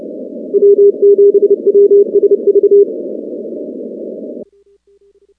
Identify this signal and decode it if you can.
Morse code, heard as a keyed tone, about 29 wpm, 400 Hz. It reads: W7WSV
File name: contests / 2019-cw-wpx-kt5j-k5pi- -2330 0-19